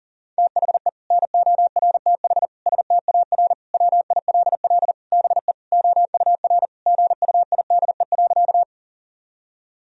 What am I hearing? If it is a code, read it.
Morse code, heard as a keyed tone, about 37 wpm, 700 Hz. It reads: THE NORTH STAR WILL BE OUR GUIDE.
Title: Sentences from Top 1000 Words - Review Part 3 - 40wpm